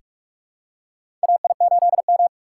Morse code, heard as a keyed tone, about 43 wpm, 700 Hz. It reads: AI8M
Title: USA Call Signs - 3x Character Spacing - 45wpm